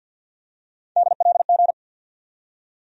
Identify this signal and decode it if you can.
Morse code, heard as a keyed tone, about 53 wpm, 700 Hz. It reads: DLG
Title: USA Call Sign Postfixes - 3 Letters 50wpm